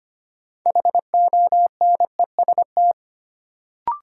Morse code, heard as a keyed tone, about 25 wpm, 700 Hz. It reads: HONEST
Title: Sets of 5 Words Encore - New Word Drill - 25wpm